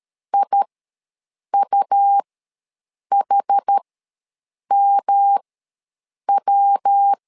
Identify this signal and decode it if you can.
Morse code, heard as a keyed tone, about 13 wpm, spaced as for 7 wpm, 800 Hz. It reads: IUHMW
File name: LEZIONI / 3 MWRO / 3 MW ws 12wpm 1816